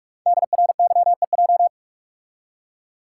Morse code, heard as a keyed tone, about 44 wpm, 700 Hz. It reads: DRYEJ
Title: Code Groups - 5 Letters - 45wpm